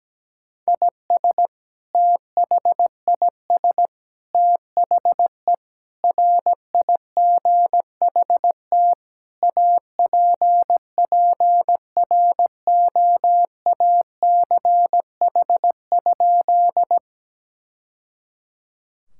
Morse code, heard as a keyed tone, about 17 wpm, 700 Hz. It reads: IS THIS THE RIGHT APPROACH?
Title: Sentences from Top 1000 Words - Review Part 6 - 17wpm